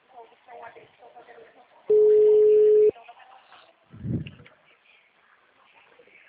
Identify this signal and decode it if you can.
Ringback tone (Europe, 425 Hz), with speech.